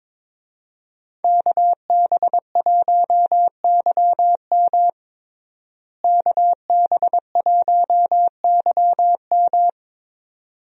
Morse code, heard as a keyed tone, about 21 wpm, 700 Hz. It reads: KB1YM KB1YM
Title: USA Call Signs - 5 Characters - Repeated 2x - 22wpm